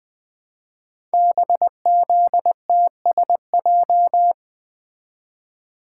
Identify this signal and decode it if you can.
Morse code, heard as a keyed tone, about 20 wpm, 700 Hz. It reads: BZTSJ
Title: Code Groups - 5 Letters - 20wpm